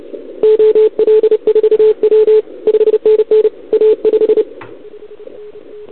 Morse code, heard as a keyed tone, about 30 wpm, 440 Hz. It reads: OL4W 5NN A5
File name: CQWWCW2021 / OL3Z